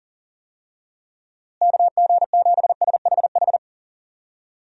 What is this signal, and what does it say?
Morse code, heard as a keyed tone, about 40 wpm, 700 Hz. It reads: KG7SHH
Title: USA Call Signs - 6 Characters - Repeated 3x - 40wpm